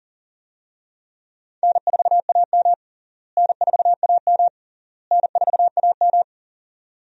Morse code, heard as a keyed tone, about 38 wpm, 700 Hz. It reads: N4AM N4AM N4AM
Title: USA Call Signs - 4 Characters - Repeated 3x - 40wpm